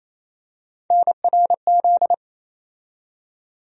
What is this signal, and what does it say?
Morse code, heard as a keyed tone, about 28 wpm, 700 Hz. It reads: NRZ
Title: USA Call Sign Postfixes - 3 Letters 28wpm